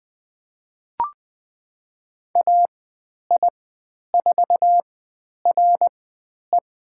Morse code, heard as a keyed tone, about 20 wpm, spaced as for 10 wpm, 700 Hz. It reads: AI4RE
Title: 095 Lesson 16g - Letter c - Call Signs 20wpm